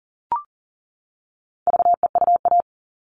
Morse code, heard as a keyed tone, about 40 wpm, 700 Hz. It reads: VEUA
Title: All Cut Numbers - 4-Digit 40wpm